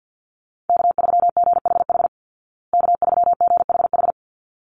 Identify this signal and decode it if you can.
Morse code, heard as a keyed tone, about 50 wpm, 700 Hz. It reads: K3GHH K3GHH
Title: Call Signs Repeated 3x - 50wpm